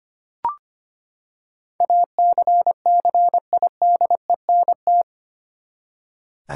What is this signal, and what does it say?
Morse code, heard as a keyed tone, about 25 wpm, 700 Hz. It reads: ACCIDENT
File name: Eight Letter Words - 25wpm